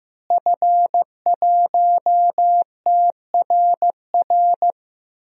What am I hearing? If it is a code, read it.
Morse code, heard as a keyed tone, about 16 wpm, 700 Hz. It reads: F1TRR
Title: DXCC-100 Call Signs - 5 Characters - Repeated 3x - 15wpm